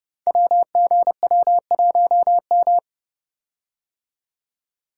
Morse code, heard as a keyed tone, about 28 wpm, 700 Hz. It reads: WGW1M
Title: Code Groups - 5 Letter:Numbers - 30wpm